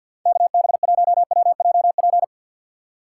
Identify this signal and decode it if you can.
Morse code, heard as a keyed tone, about 49 wpm, 700 Hz. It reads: KB1WJP
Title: USA Call Signs - 6 Characters - 4x Character Spacing - 50wpm